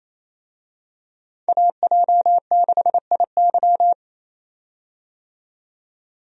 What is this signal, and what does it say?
Morse code, heard as a keyed tone, about 27 wpm, 700 Hz. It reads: AJ6IY
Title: USA Call Signs - 5 Characters - 28wpm